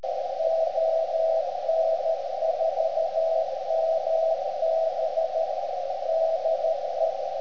Morse code, heard as a keyed tone, about 13 wpm, 650 Hz, with radio noise. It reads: ON4MB DE
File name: ft8-vs-cw / 90-to-150 CW 300hz